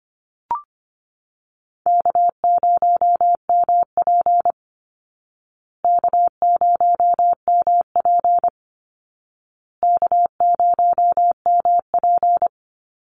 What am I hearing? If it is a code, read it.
Morse code, heard as a keyed tone, about 25 wpm, 700 Hz. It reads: K0MP K0MP K0MP
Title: Call Signs Repeated 2x - 25wpm